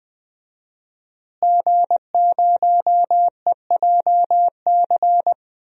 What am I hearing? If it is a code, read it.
Morse code, heard as a keyed tone, about 20 wpm, 700 Hz. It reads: G0EJC
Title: DXCC-100 Call Signs - 5 Characters - Repeated 3x - 20wpm